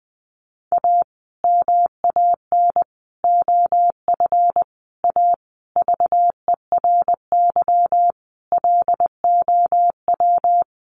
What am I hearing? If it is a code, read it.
Morse code, heard as a keyed tone, about 20 wpm, 700 Hz. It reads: A MAN OF A VERY LOW
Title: Sentences from Top 1200 Words 20wpm